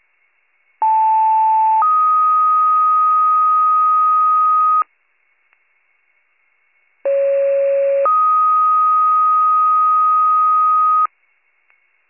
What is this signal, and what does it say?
Two-tone page, heard as a one-second tone, then a three-second tone, about 880 then 1250 Hz, about 550 then 1250 Hz. It is heard from 0.8 to 11.1 s.